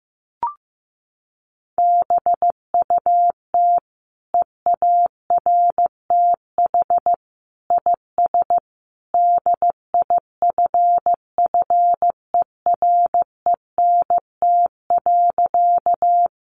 Morse code, heard as a keyed tone, about 14 wpm, 700 Hz. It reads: BUT EARTH IS DIFFERENT.
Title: Sentences from Top 600 Words 15wpm